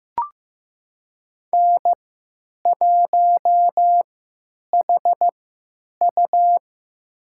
Morse code, heard as a keyed tone, about 15 wpm, spaced as for 9 wpm, 700 Hz. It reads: N1HU
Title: USA Call Signs - 4 Characters - 3x Character Spacing - 15wpm